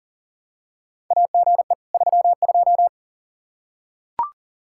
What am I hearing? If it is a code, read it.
Morse code, heard as a keyed tone, about 39 wpm, 700 Hz. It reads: AGE 32
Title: Practice QSO - Exchange 3 - More Info - 40wpm